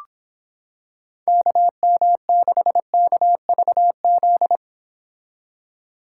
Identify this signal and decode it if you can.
Morse code, heard as a keyed tone, about 26 wpm, 700 Hz. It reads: KM6KVZ